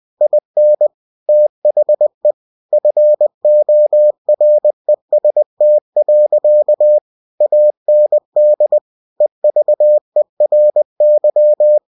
Morse code, heard as a keyed tone, about 20 wpm, 600 Hz. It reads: IN THE FOREST. AND EVERY